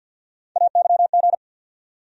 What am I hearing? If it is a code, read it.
Morse code, heard as a keyed tone, about 47 wpm, 700 Hz. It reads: AYG
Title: USA Call Sign Postfixes - 3 Letters 50wpm